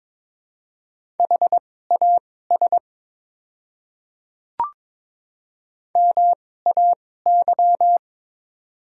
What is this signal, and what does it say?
Morse code, heard as a keyed tone, about 22 wpm, spaced as for 16 wpm, 700 Hz. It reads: HAS MAY
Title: 2x Character Spacing - Top 100 Words - 22wpm